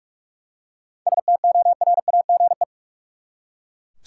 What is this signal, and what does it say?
Morse code, heard as a keyed tone, about 41 wpm, 700 Hz. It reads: STORAGE
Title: Seven Letter Words - 45wpm